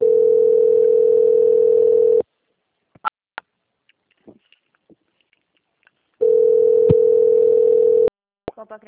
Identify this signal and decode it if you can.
Ringback tone (North America), 440 + 480 Hz.
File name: out- - - - - .11833